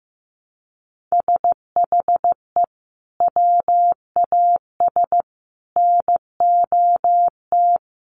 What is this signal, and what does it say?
Morse code, heard as a keyed tone, about 15 wpm, 700 Hz. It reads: SHE WAS NOT
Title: Sets of 5 Words Encore2 15wpm